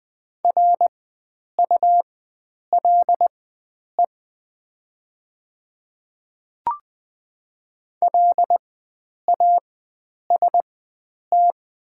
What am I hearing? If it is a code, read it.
Morse code, heard as a keyed tone, about 20 wpm, spaced as for 9 wpm, 700 Hz. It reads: RULE LAST
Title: 4x Character Spacing - Top 500 Words - 20wpm